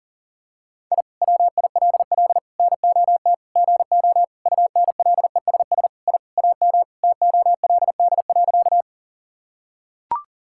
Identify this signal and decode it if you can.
Morse code, heard as a keyed tone, about 40 wpm, 700 Hz. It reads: I WILL NOT GO UNLESS I AM TOLD.
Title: Sentences from Top 1000 Words - Review Part 6 - 40wpm